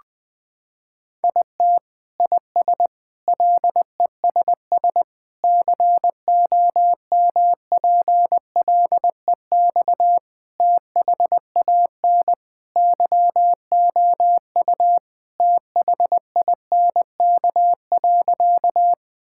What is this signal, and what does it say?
Morse code, heard as a keyed tone, about 20 wpm, 700 Hz. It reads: IT IS LESS COMPLEX THAN YOU THINK.
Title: Sentences from Top 1000 Words - Review Part 6 - 20wpm